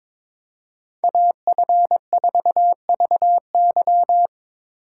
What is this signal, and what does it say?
Morse code, heard as a keyed tone, about 25 wpm, 700 Hz. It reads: AF4VY
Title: Speed Racing - USA 2x2 Call Signs 22wpm